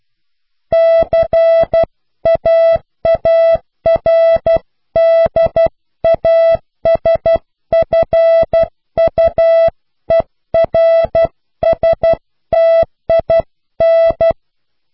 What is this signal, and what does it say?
Morse code, heard as a keyed tone, about 12 wpm, 660 Hz. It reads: CAARDASFUERSTIN
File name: cw / Pseudo woorden (15)